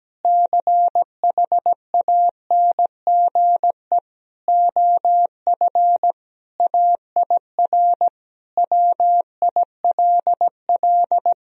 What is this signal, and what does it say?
Morse code, heard as a keyed tone, about 17 wpm, 700 Hz. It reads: CHANGE OF AIR WILL
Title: Sentences from Top 1000 Words - Review Part 1 - 17wpm